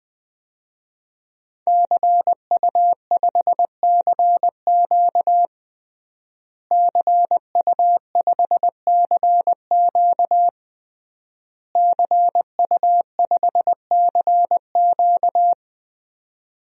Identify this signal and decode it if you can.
Morse code, heard as a keyed tone, about 20 wpm, 700 Hz. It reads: CU5CQ CU5CQ CU5CQ
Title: DXCC-100 Call Signs - 5 Characters - Repeated 3x - 20wpm